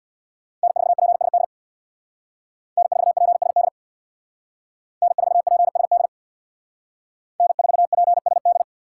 Morse code, heard as a keyed tone, about 58 wpm, 700 Hz. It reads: N4PSD N4PSD N4PSD N4PSD
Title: Speed Racing - USA 1x3 Call Signs 50wpm